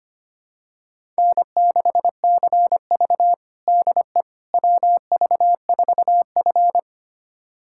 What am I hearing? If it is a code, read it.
Morse code, heard as a keyed tone, about 25 wpm, 700 Hz. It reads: N6CV DE WV4F
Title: USA Callsign DE Callsign - 4 Characters - 25wpm